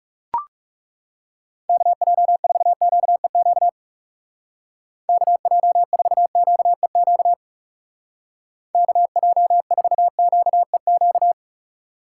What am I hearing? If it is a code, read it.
Morse code, heard as a keyed tone, about 35 wpm, 700 Hz. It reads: KJ4QEQ KJ4QEQ KJ4QEQ
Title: Speed Racing - USA 2x3 Call Signs 30wpm